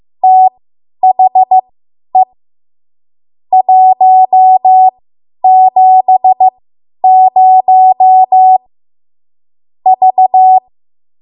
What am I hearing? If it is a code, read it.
Morse code, heard as a keyed tone, about 15 wpm, spaced as for 10 wpm, 750 Hz. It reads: THE 170 V
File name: Archive / 10 WPM / 10WPM